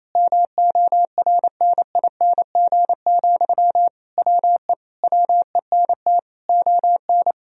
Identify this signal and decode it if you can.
Morse code, heard as a keyed tone, about 28 wpm, 700 Hz. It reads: MORNING, WE WENT ON
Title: Sentences from Top 1000 Words - Review Part 4 - 28wpm